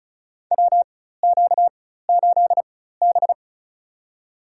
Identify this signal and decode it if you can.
Morse code, heard as a keyed tone, about 36 wpm, spaced as for 16 wpm, 700 Hz. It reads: WQ8B
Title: USA Call Signs - 4x Character Spacing - 35wpm